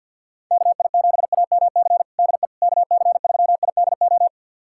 Morse code, heard as a keyed tone, about 50 wpm, 700 Hz. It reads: KI7AMC DE KC3IDO
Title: USA Callsign DE Callsign 50wpm